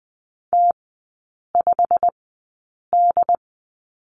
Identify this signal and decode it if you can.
Morse code, heard as a keyed tone, about 20 wpm, spaced as for 8 wpm, 700 Hz. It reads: T5D
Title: 093 Lesson 16e - Letter c - Three Characters 20wpm